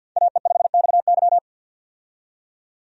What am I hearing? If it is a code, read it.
Morse code, heard as a keyed tone, about 46 wpm, 700 Hz. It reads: AE5XY
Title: USA Call Signs - 5 Characters - 50wpm